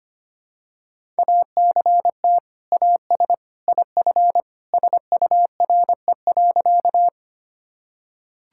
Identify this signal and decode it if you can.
Morse code, heard as a keyed tone, about 25 wpm, 700 Hz. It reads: ACT AS IF SURE.
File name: practice / 25wpm / Sentences with 2-Word Phrases - 25wpm